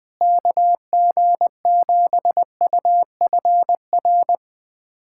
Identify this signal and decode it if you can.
Morse code, heard as a keyed tone, about 19 wpm, 700 Hz. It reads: KG7UFR
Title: USA Callsign DE Callsign 20wpm